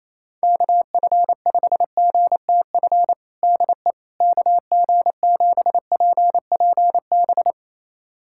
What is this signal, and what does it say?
Morse code, heard as a keyed tone, about 28 wpm, 700 Hz. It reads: KF5GTF DE KG7PPB
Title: USA Callsign DE Callsign 28wpm